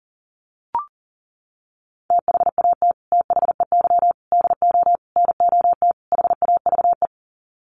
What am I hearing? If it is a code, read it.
Morse code, heard as a keyed tone, about 40 wpm, 700 Hz. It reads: THAT THEY DO NOT HAVE
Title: Sets of 5 Words Encore2 40wpm